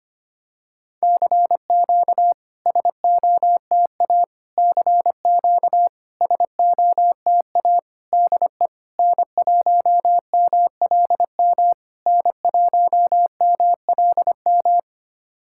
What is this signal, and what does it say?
Morse code, heard as a keyed tone, about 25 wpm, 700 Hz. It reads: CQ SOTA CQ SOTA DE N1MLM N1MLM
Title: SOTA QSO Practice 25wpm